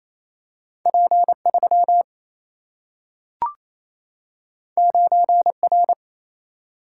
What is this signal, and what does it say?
Morse code, heard as a keyed tone, about 28 wpm, 700 Hz. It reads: P 3 9 R
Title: Single-Letter Single-Number 28wpm